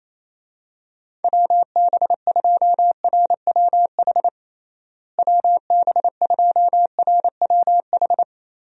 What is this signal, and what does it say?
Morse code, heard as a keyed tone, about 28 wpm, 700 Hz. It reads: WB2RWH WB2RWH
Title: USA Call Signs - 6 Characters - Repeated 3x - 28wpm